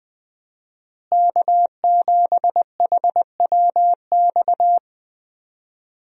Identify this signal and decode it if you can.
Morse code, heard as a keyed tone, about 20 wpm, 700 Hz. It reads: K7HWX
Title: USA Call Signs - 5 Characters - Repeated 3x - 20wpm